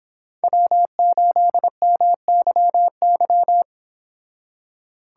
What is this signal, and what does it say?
Morse code, heard as a keyed tone, about 26 wpm, 700 Hz. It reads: W8MYY